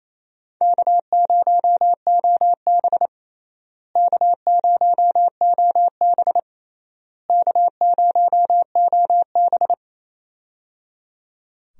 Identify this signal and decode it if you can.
Morse code, heard as a keyed tone, about 27 wpm, 700 Hz. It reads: K0OB K0OB K0OB
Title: USA Call Signs - 4 Characters - Repeated 3x - 28wpm